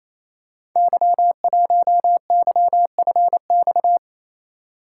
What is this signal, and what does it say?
Morse code, heard as a keyed tone, about 29 wpm, 700 Hz. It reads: Y1YFX
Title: Code Groups - 5 Letter:Numbers - 28wpm